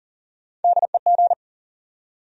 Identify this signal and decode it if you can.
Morse code, heard as a keyed tone, about 40 wpm, 700 Hz. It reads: DEG
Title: USA Call Sign Postfixes - 3 Letters 40wpm